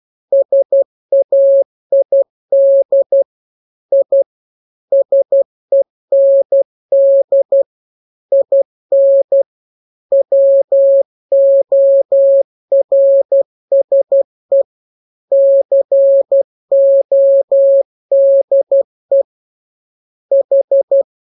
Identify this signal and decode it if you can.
Morse code, heard as a keyed tone, about 12 wpm, 550 Hz. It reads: SAID I SEND IN WORSE CODE H